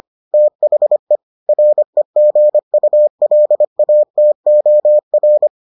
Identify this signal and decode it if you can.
Morse code, heard as a keyed tone, about 25 wpm, 600 Hz. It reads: THE REGULATOR